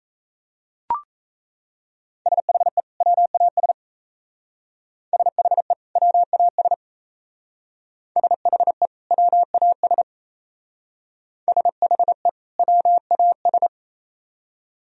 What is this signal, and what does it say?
Morse code, heard as a keyed tone, about 35 wpm, 690 Hz. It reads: SHE WAS SHE WAS SHE WAS SHE WAS